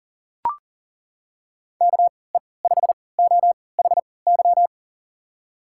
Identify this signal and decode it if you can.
Morse code, heard as a keyed tone, about 39 wpm, spaced as for 22 wpm, 700 Hz. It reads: KE5OHY
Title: USA Call Signs - 6 Characters - 3x Character Spacing - 40wpm